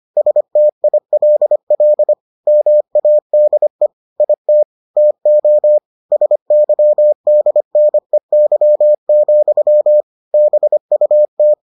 Morse code, heard as a keyed tone, about 25 wpm, 600 Hz. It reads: STILL MADE IT TO SYDNEY, BUT